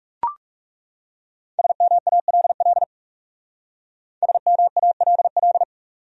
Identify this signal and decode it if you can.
Morse code, heard as a keyed tone, about 36 wpm, 700 Hz. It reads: SMALL SMALL
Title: Speed-Racing Top 500 Words 30wpm